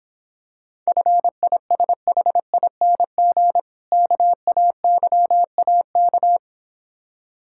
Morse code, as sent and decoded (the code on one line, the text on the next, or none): ..-. .. ... .... .. -. --. / -.- .- -.-- .- -.-
FISHING KAYAK